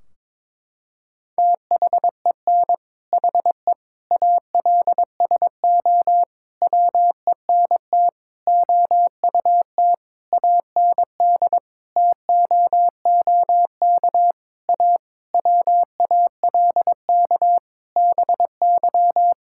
Morse code, heard as a keyed tone, about 22 wpm, 700 Hz. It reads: THEN HE ALSO WENT OUT AND TOOK A WALK BY
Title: Sentences from Top 1600 Words 22wpm